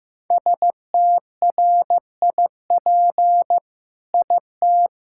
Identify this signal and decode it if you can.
Morse code, heard as a keyed tone, about 15 wpm, 700 Hz. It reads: STRIP IT
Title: Sentences from Top 1600 Words 15wpm